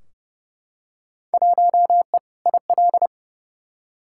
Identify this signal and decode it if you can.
Morse code, heard as a keyed tone, about 30 wpm, 700 Hz. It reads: 1E IL
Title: Field Day Exchanges - 30wpm